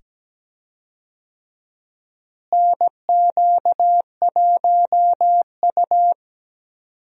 Morse code, heard as a keyed tone, about 17 wpm, 700 Hz. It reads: NQ1U
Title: USA Call Signs - 4 Characters - Repeated 2x - 17wpm